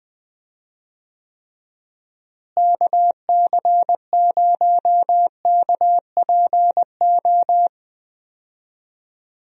Morse code, heard as a keyed tone, about 20 wpm, 700 Hz. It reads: KC0KPO
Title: USA Call Signs - 6 Characters - Repeated 3x - 20wpm